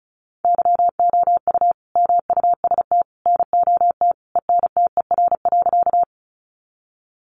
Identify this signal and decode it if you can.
Morse code, heard as a keyed tone, about 34 wpm, 700 Hz. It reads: YOU MUST NOT ENTER.
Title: Sentences from Top 1300 Words 35wpm